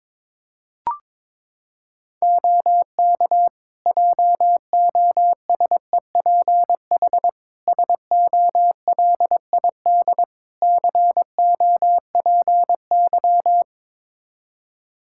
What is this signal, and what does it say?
Morse code, heard as a keyed tone, about 22 wpm, 700 Hz. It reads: OK JOSEPH SOLID COPY
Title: Practice QSO - Exchange 3 - More Info - 22wpm